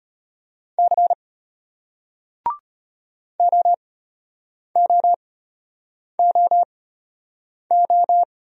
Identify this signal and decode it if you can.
Morse code, heard as a keyed tone, about 33 wpm, spaced as for 8 wpm, 700 Hz. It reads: C OOOO